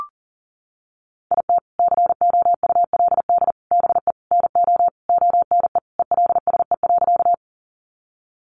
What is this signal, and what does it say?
Morse code, heard as a keyed tone, about 40 wpm, 700 Hz. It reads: IT COULD BE NO ONE ELSE.
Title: Sentences from Top 800 Words 40wpm